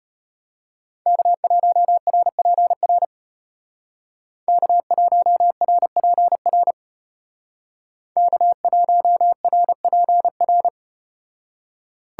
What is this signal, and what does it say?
Morse code, heard as a keyed tone, about 34 wpm, 700 Hz. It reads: K1RPR K1RPR K1RPR